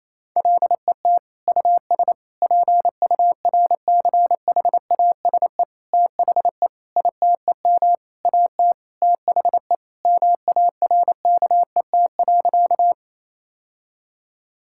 Morse code, heard as a keyed tone, about 28 wpm, 700 Hz. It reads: LET US PURCHASE THE ITEM AT THE MARKET.
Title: Sentences from Top 1000 Words - Review Part 6 - 28wpm